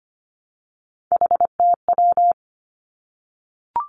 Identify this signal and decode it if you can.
Morse code, heard as a keyed tone, about 25 wpm, 700 Hz. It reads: HTW
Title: Three Challenging Letter-Number Combinations 25wpm